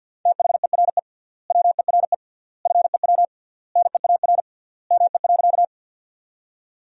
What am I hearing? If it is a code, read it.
Morse code, heard as a keyed tone, about 50 wpm, 700 Hz. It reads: THERE WERE FEW NEAR ME.
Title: 3x Word-Spacing - Sentences from Top 300 Words - 50wpm